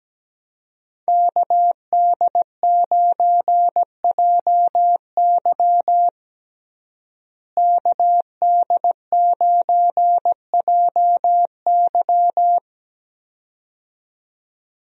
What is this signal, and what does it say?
Morse code, heard as a keyed tone, about 17 wpm, 700 Hz. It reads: KD9JY KD9JY
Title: USA Call Signs - 5 Characters - Repeated 2x - 17wpm